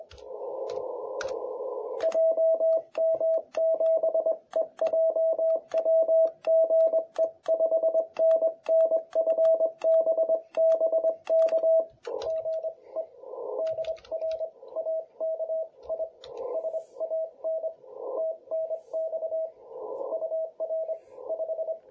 Morse code, heard as a keyed tone, about 22 wpm, 650 Hz, with radio noise. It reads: JM7EJW GE 5NN FB BK GE DR AKI SAN TNX UR 5
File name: jh8cbh / 2021cw